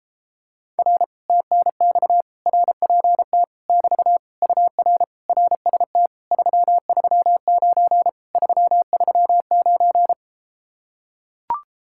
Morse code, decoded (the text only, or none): R TNX RPT <BT> UR RST 339 339